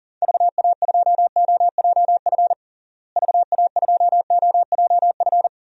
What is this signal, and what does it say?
Morse code, heard as a keyed tone, about 37 wpm, 700 Hz. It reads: VA2OJF VA2OJF
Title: DXCC-100 Call Signs - 6 Characters - Repeated 3x - 40wpm